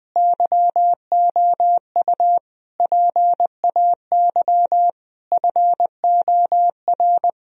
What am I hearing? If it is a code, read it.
Morse code, heard as a keyed tone, about 20 wpm, 700 Hz. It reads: YOU PAY FOR
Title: Sentences from Top 1000 Words - Review Part 3 - 20wpm